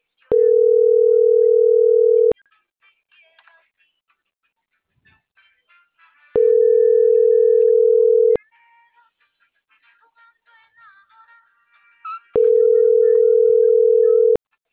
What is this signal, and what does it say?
Ringback tone (North America), 440 + 480 Hz.